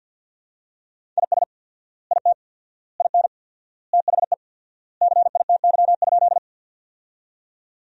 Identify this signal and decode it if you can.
Morse code, heard as a keyed tone, about 47 wpm, 700 Hz. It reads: IS IT IN THE CITY?
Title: 4x Word-Spacing - Sentences from Top 300 Words - 50wpm